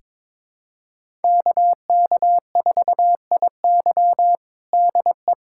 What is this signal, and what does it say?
Morse code, heard as a keyed tone, about 22 wpm, 700 Hz. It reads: KK4IY DE
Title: USA Callsign DE Callsign - 5 Characters - 22wpm